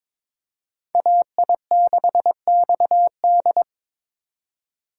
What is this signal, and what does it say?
Morse code, heard as a keyed tone, about 22 wpm, 700 Hz. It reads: AI6XD
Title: USA Call Signs - 5 Characters - 3x Character Spacing - 22wpm